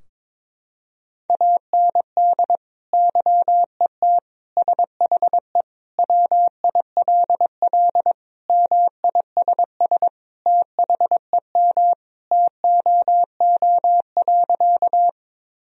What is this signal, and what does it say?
Morse code, heard as a keyed tone, about 22 wpm, 700 Hz. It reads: AND YET SHE WILL MISS THEM TOO.
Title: Sentences from Top 1000 Words - Review Part 3 - 22wpm